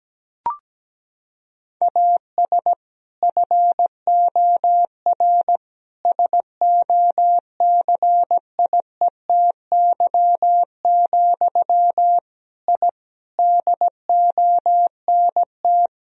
Morse code, heard as a keyed tone, about 17 wpm, 700 Hz. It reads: AS FOR SOCIETY, I DONT
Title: Sentences from Top 1000 Words - Review Part 4 - 17wpm